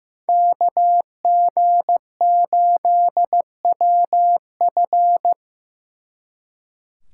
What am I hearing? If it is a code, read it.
Morse code, heard as a keyed tone, about 18 wpm, 700 Hz. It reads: KG8WF
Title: Speed Racing - USA 2x2 Call Signs 15wpm